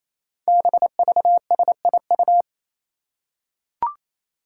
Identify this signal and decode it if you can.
Morse code, heard as a keyed tone, about 28 wpm, 700 Hz. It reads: BVSIU
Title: Five Challenging Letter-Number Combinations 28wpm